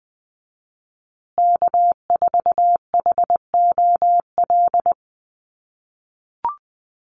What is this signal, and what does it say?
Morse code, heard as a keyed tone, about 20 wpm, 700 Hz. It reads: K4HOL
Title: USA Call Signs Repeated 3x 20wpm